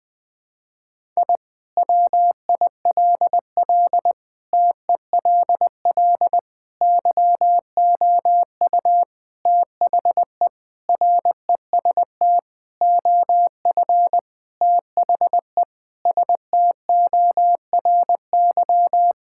Morse code, heard as a keyed tone, about 20 wpm, 700 Hz. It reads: I WILL TELL YOU THE REST OF THE STORY
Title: Sentences with 4-Word Phrases - Spoken Beginning - 20wpm